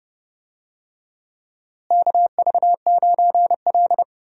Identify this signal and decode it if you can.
Morse code, heard as a keyed tone, about 30 wpm, 700 Hz. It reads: KV9L
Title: USA Call Signs - 4 Characters - Repeated 2x - 30wpm